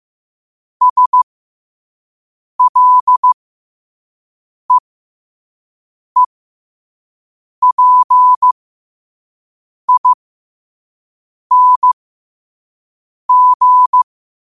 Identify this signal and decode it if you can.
Morse code, heard as a keyed tone, about 15 wpm, spaced as for 5 wpm, 1000 Hz. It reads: SLEEPING
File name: qotdfiles / 5wpm